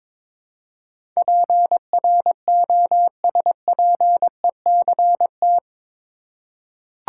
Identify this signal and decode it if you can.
Morse code, heard as a keyed tone, about 22 wpm, 700 Hz. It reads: PROSPECT
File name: practice / 22wpm / Eight Letter Words - 22wpm